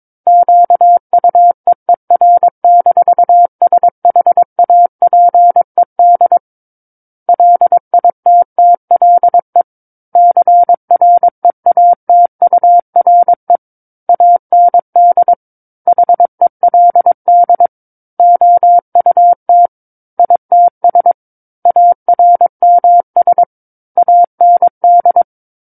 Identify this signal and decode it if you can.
Morse code, heard as a keyed tone, about 22 wpm, 700 Hz. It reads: QUEER-SHAPED LITTLE CREATURE AND HELD OUT ITS ARMS AND